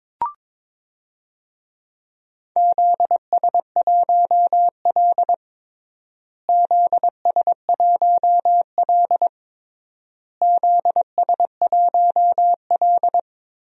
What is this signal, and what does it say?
Morse code, heard as a keyed tone, about 23 wpm, 700 Hz. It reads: ZS1L ZS1L ZS1L
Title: DXCC-100 Call Signs - 4 Characters - Repeated 3x - 22wpm